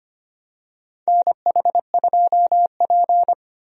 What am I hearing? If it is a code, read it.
Morse code, heard as a keyed tone, about 25 wpm, 700 Hz. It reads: NH2P
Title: USA Call Signs - 4 Characters - 25wpm